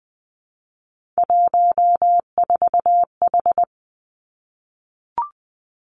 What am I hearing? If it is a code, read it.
Morse code, heard as a keyed tone, about 19 wpm, 700 Hz. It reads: 14H
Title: Three Challenging Letter-Number Combinations 20wpm